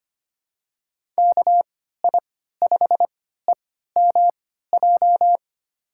Morse code, heard as a keyed tone, about 25 wpm, spaced as for 14 wpm, 700 Hz. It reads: KI5EMJ
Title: USA Call Signs - 6 Characters - 3x Character Spacing - 25wpm